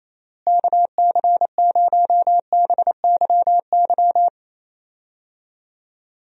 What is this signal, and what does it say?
Morse code, heard as a keyed tone, about 28 wpm, 700 Hz. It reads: KC0BYY
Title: USA Call Signs - 6 Characters - Repeated 2x - 28wpm